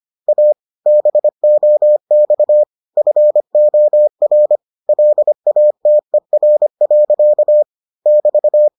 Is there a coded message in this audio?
Yes: Morse code from 0.3 to 8.7 s, heard as a keyed tone, about 25 wpm, 600 Hz. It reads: A BOX FOR LATER. <BT>